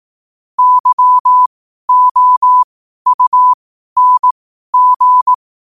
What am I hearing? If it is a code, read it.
Morse code, heard as a keyed tone, about 18 wpm, spaced as for 12 wpm, 1000 Hz. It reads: YOUNG
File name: qotdfiles / 12wpm